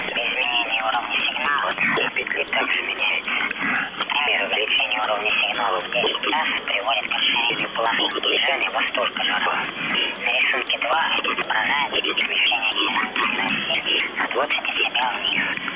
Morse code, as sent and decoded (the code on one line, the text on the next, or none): none